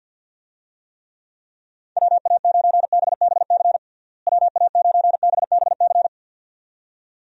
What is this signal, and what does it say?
Morse code, heard as a keyed tone, about 52 wpm, 700 Hz. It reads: WA9BBC WA9BBC
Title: USA Call Signs - 6 Characters - Repeated 2x - 50wpm